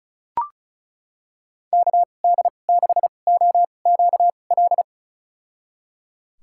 Morse code, heard as a keyed tone, about 36 wpm, spaced as for 25 wpm, 700 Hz. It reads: KD6OQL
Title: USA Call Signs - 6 Characters - 2x Character Spacing - 35wpm